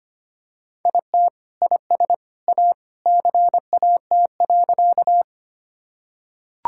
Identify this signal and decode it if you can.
Morse code, heard as a keyed tone, about 25 wpm, 700 Hz. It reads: IT IS A CAT.
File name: practice / 25wpm / Sentences with 2-Word Phrases - 25wpm